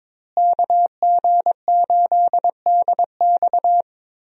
Morse code, heard as a keyed tone, about 22 wpm, 700 Hz. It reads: KG8DX
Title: USA Call Signs - 5 Characters - Repeated 2x - 22wpm